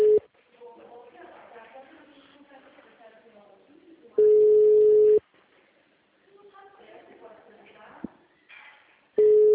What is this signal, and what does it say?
Ringback tone (Europe, 425 Hz), with speech.